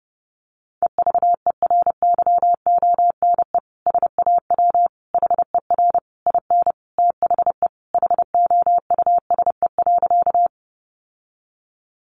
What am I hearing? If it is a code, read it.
Morse code, heard as a keyed tone, about 30 wpm, 700 Hz. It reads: EVERYONE SAW HER IN THE HOUSE.
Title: Sentences from Top 800 Words 30wpm